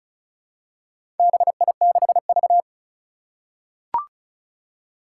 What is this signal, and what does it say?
Morse code, heard as a keyed tone, about 36 wpm, 700 Hz. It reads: BI6V
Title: Four Challenging Letter-Number Combinations 35wpm